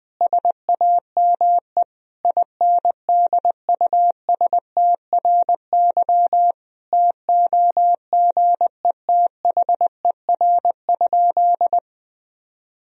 Morse code, heard as a keyed tone, about 21 wpm, 700 Hz. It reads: SAME INDUSTRY TOGETHER?
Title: Sentences from Top 1000 Words - Review Part 4 - 20wpm